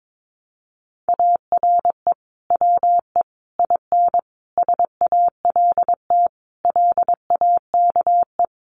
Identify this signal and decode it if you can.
Morse code, heard as a keyed tone, about 22 wpm, 700 Hz. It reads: ARE WE IN SALT LAKE
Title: Sentences from Top 1100 Words 22wpm